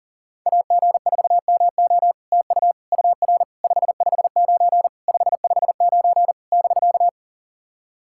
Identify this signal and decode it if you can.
Morse code, heard as a keyed tone, about 44 wpm, 700 Hz. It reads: AG4MO TU UR 559 559 <BK>
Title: SOTA QSO Practice 40wpm